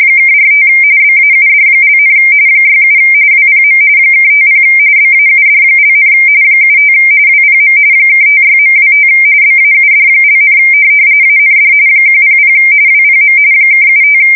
RTTY, about 45.45 baud. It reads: (45 BAUD), 67 WPM (50 BAUD), 75 WPM (56.25 BAUD), AND 100 WPM (75
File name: fsk170 / teletype-RTTY